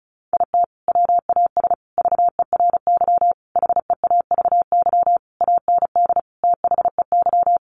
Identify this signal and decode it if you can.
Morse code, heard as a keyed tone, about 34 wpm, 700 Hz. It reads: IT WAS VERY HEAVY AND THEY
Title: Sentences from Top 1100 Words 35wpm